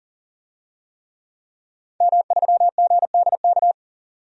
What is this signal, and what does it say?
Morse code, heard as a keyed tone, about 40 wpm, 700 Hz. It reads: M3GDK
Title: DXCC-100 Call Signs - 5 Characters - 40wpm